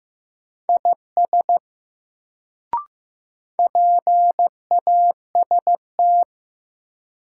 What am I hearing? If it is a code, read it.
Morse code, heard as a keyed tone, about 16 wpm, 700 Hz. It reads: IS PAST
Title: Sets of 2 Words - New Word Drill - 15wpm